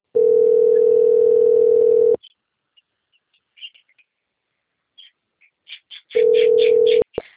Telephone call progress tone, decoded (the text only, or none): ringback tone